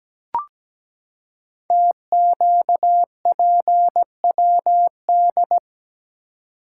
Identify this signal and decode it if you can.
Morse code, heard as a keyed tone, about 17 wpm, 700 Hz. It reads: TQPWD
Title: Code Groups - 5 Letters - 17wpm